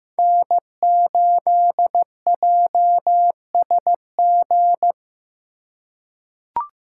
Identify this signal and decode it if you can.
Morse code, heard as a keyed tone, about 15 wpm, 700 Hz. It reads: N8JSG
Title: USA Call Signs - 5 Characters - 4x Character Spacing - 15wpm